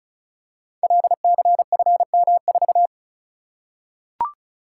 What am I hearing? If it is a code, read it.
Morse code, heard as a keyed tone, about 36 wpm, 690 Hz. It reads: LCFM4
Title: Code Groups - 5 Letter:Numbers - 35wpm